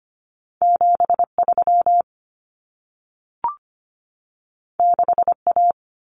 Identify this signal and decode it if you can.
Morse code, heard as a keyed tone, about 25 wpm, 700 Hz. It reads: 73 6A
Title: Common Cut Numbers - 2-Digit 25wpm